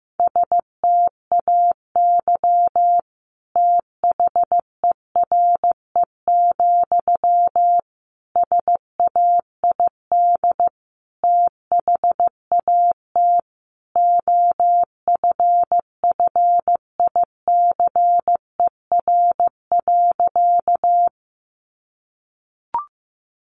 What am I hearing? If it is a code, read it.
Morse code, heard as a keyed tone, about 15 wpm, 700 Hz. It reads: STAY THERE, SAID THAT OFFICER.
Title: Sentences from Top 1200 Words 15wpm